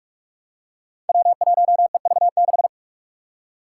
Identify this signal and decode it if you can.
Morse code, heard as a keyed tone, about 42 wpm, 700 Hz. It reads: W1EV6
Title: Five Challenging Letter-Number Combinations 45wpm